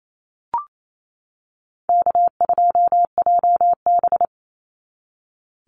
Morse code, heard as a keyed tone, about 28 wpm, 700 Hz. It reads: K2JB
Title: NA SOTA Chaser Callsigns 28wpm